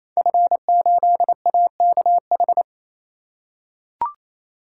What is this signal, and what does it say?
Morse code, heard as a keyed tone, about 27 wpm, 700 Hz. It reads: F8AKH
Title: Code Groups - 5 Letter:Numbers - 28wpm